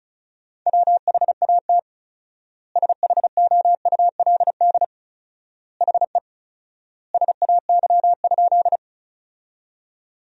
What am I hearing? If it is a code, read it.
Morse code, heard as a keyed tone, about 34 wpm, 700 Hz. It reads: WHAT SHOULD HE SAY?
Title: 4x Word-Spacing - Sentences from Top 300 Words - 35wpm